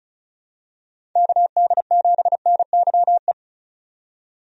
Morse code, heard as a keyed tone, about 34 wpm, 700 Hz. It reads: KD7NYE
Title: USA Call Signs - 6 Characters - 35wpm